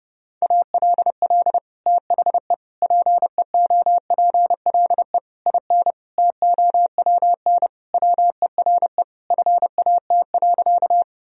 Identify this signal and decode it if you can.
Morse code, heard as a keyed tone, about 30 wpm, 700 Hz. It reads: ALL THE PEOPLE IN TOWN WERE FAT.
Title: Sentences from Top 1000 Words - Review Part 4 - 30wpm